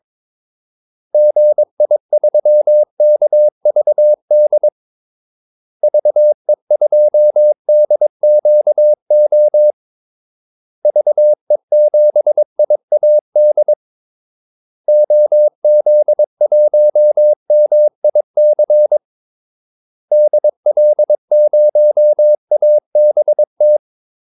Morse code, heard as a keyed tone, about 22 wpm, 600 Hz. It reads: GI3KVD VE2DQO VE7IAD OZ1MIC DL0ABT
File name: Callsigns-22wpm